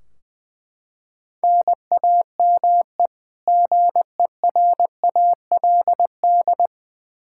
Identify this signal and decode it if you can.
Morse code, heard as a keyed tone, about 20 wpm, 700 Hz. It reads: NAME GERALD
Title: Practice QSO - Exchange 1 - Beginning - 20wpm